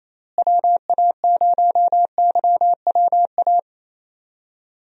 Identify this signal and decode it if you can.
Morse code, heard as a keyed tone, about 28 wpm, 700 Hz. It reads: WA0YWA
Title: USA Call Signs - 6 Characters - Repeated 2x - 28wpm